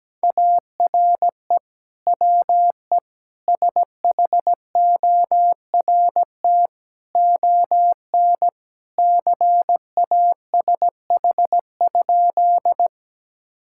Morse code, heard as a keyed tone, about 17 wpm, 700 Hz. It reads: ARE WE SHORT ON CASH?
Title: Sentences from Top 1000 Words - Review Part 5 - 17wpm